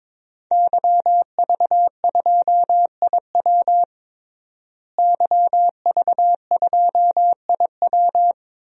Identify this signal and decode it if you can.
Morse code, heard as a keyed tone, about 22 wpm, 700 Hz. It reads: YV2IW YV2IW
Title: DXCC-100 Call Signs - 5 Characters - Repeated 2x - 22wpm